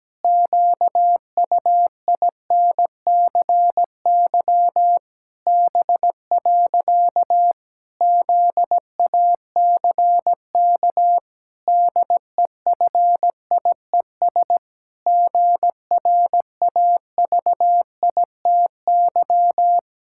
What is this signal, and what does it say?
Morse code, heard as a keyed tone, about 17 wpm, 700 Hz. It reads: QUINCY B. ZACK DEFIES GRAVITY
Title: Pangram Challenge v2 - 17wpm